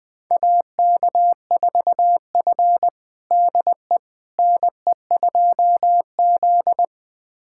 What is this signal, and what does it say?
Morse code, heard as a keyed tone, about 20 wpm, 700 Hz. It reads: AK4F DE NE2Z
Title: USA Callsign DE Callsign - 4 Characters - 20wpm